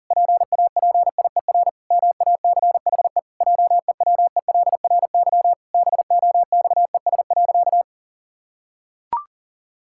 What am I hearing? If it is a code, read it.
Morse code, heard as a keyed tone, about 40 wpm, 700 Hz. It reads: PAPIER MACHE JEWELRY BOXES.
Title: Pangram Challenge v2 - 40wpm